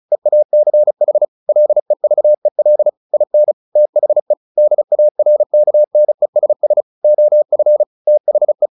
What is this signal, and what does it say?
Morse code, heard as a keyed tone, about 36 wpm, 600 Hz. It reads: EACH LEVEL IN THE DARKNESS OF THE
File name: the old man and the sea / the old man and the sea 1 35wpm